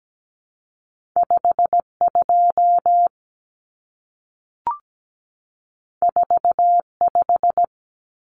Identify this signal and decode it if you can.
Morse code, heard as a keyed tone, about 17 wpm, 700 Hz. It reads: 52 45
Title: Numbers - 2-Digit 17wpm